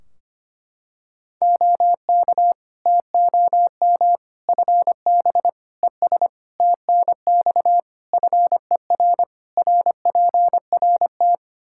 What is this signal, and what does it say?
Morse code, heard as a keyed tone, about 25 wpm, 700 Hz. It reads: OK TOM FB ES TNX FER RPRT
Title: Practice QSO - Exchange 2 - Info - 25wpm